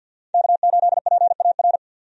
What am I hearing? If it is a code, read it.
Morse code, heard as a keyed tone, about 51 wpm, 700 Hz. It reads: K8PAR
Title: USA Call Signs - 5 Characters - 50wpm